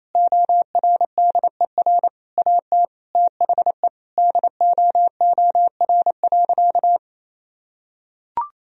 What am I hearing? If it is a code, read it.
Morse code, heard as a keyed tone, about 28 wpm, 700 Hz. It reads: ORDER AT THE DOOR.
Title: Sentences from Top 1000 Words - Review Part 2 - 28wpm